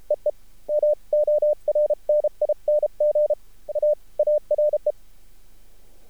Morse code, heard as a keyed tone, about 33 wpm, spaced as for 29 wpm, 600 Hz. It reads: EE MORNING AARE
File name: archived / CRSnet